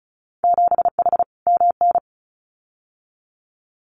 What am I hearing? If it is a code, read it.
Morse code, heard as a keyed tone, about 34 wpm, 700 Hz. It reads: 7H MN
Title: Winter Field Day Exchanges Encore 35wpm